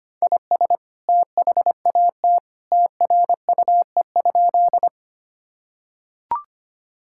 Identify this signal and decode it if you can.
Morse code, heard as a keyed tone, about 25 wpm, 700 Hz. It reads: IS THAT TRUE?
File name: practice / 25wpm / Sentences with 2-Word Phrases - 25wpm